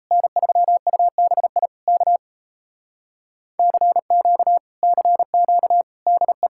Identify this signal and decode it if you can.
Morse code, heard as a keyed tone, about 37 wpm, 700 Hz. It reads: N3UBI K CQ CQ DE